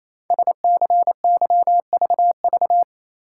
Morse code, heard as a keyed tone, about 28 wpm, 700 Hz. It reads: SCYVV
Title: Code Groups - 5 Letters - 28wpm